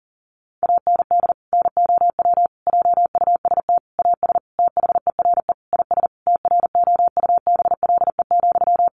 Morse code, heard as a keyed tone, about 40 wpm, 700 Hz. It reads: AND NOW JUST AS THERE IS TROUBLE,
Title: Sentences from Top 1200 Words 40wpm